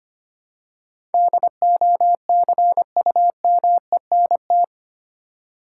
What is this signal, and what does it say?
Morse code, heard as a keyed tone, about 25 wpm, 700 Hz. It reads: DOCUMENT
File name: Eight Letter Words - 25wpm